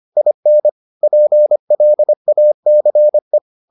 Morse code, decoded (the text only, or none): IN PLACE